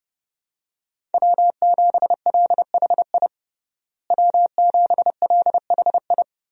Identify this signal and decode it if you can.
Morse code, heard as a keyed tone, about 30 wpm, 700 Hz. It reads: W7LHI W7LHI
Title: USA Call Signs - 5 Characters - Repeated 3x - 30wpm